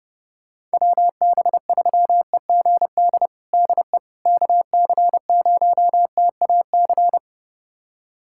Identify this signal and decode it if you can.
Morse code, heard as a keyed tone, about 32 wpm, 700 Hz. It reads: WB3EGD DE KC0TAC
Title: USA Callsign DE Callsign - 6 Characters - 30wpm